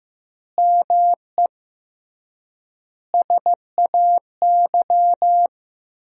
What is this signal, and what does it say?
Morse code, heard as a keyed tone, about 15 wpm, 700 Hz. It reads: ME SAY
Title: 3x Word-Spacing - Sentences from Top 300 Words - 15wpm